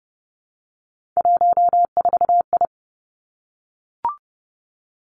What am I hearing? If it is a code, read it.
Morse code, heard as a keyed tone, about 30 wpm, 700 Hz. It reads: 14I
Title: Three Challenging Letter-Number Combinations 30wpm